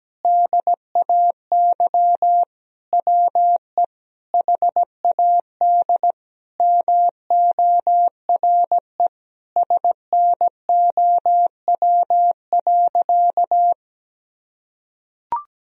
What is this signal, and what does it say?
Morse code, heard as a keyed tone, about 17 wpm, 700 Hz. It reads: DAY WE HAD MORE SNOW.
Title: Sentences from Top 1000 Words - Review Part 3 - 17wpm